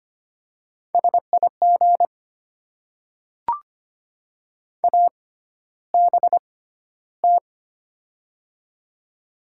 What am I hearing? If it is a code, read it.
Morse code, heard as a keyed tone, about 25 wpm, spaced as for 13 wpm, 700 Hz. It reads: SIG ABT
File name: Top 90 QSO Elements - 6x Character Spacing - 25wpm